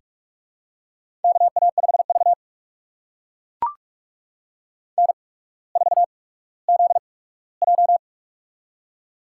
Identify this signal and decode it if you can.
Morse code, heard as a keyed tone, about 44 wpm, spaced as for 19 wpm, 700 Hz. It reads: KA5V N4ZJ
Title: USA Call Signs - 4 Characters - 8x Character Spacing - 45wpm